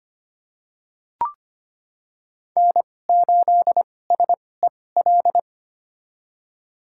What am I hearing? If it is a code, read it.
Morse code, heard as a keyed tone, about 25 wpm, spaced as for 18 wpm, 700 Hz. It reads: N8SEL
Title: USA Call Signs - 5 Characters - 2x Character Spacing - 25wpm